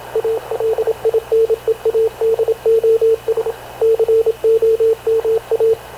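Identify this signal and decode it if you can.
Morse code, heard as a keyed tone, about 27 wpm, 450 Hz, with radio noise. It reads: ALINEADOS COMA
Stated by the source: Havana Radio / CLA sending Spanish press (PX) in Morse, recorded by Radio Officer Paul Dumesnil onboard CNAV QUEST/CZDO.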